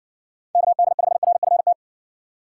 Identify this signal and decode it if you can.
Morse code, heard as a keyed tone, about 60 wpm, 700 Hz. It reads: KD5RFT